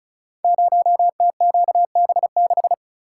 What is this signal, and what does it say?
Morse code, heard as a keyed tone, about 36 wpm, 690 Hz. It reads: 0TQB6
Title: Code Groups - 5 Letter:Numbers - 35wpm